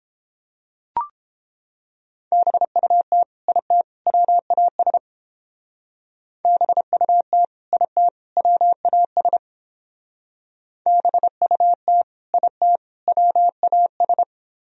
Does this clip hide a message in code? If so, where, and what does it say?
Yes: Morse code from 2.3 to 14.2 s, heard as a keyed tone, about 29 wpm, 700 Hz. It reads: BUT IT WAS BUT IT WAS BUT IT WAS